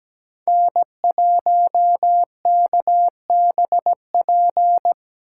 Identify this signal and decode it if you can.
Morse code, heard as a keyed tone, about 17 wpm, 700 Hz. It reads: N1KBP
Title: USA Call Signs - 3x Character Spacing - 17wpm